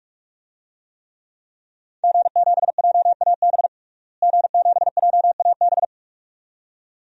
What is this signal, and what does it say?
Morse code, heard as a keyed tone, about 41 wpm, 700 Hz. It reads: G7JAB G7JAB
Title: DXCC-100 Call Signs - 5 Characters - Repeated 2x - 45wpm